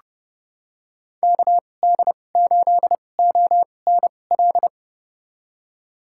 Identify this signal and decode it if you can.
Morse code, heard as a keyed tone, about 30 wpm, spaced as for 22 wpm, 700 Hz. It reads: KD8ONL
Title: USA Call Signs - 6 Characters - 2x Character Spacing - 30wpm